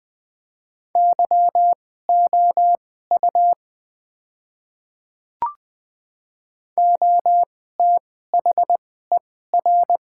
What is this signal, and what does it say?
Morse code, heard as a keyed tone, about 20 wpm, spaced as for 14 wpm, 700 Hz. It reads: YOU OTHER
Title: 2x Character Spacing - Top 100 Words - 20wpm